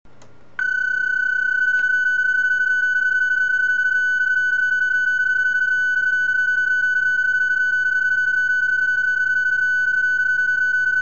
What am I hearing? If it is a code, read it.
PSK31, about 31.25 baud.